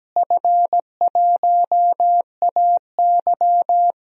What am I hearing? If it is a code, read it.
Morse code, heard as a keyed tone, about 17 wpm, 700 Hz. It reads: F1AY
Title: DXCC-100 Call Signs - 4 Characters - Repeated 3x - 17wpm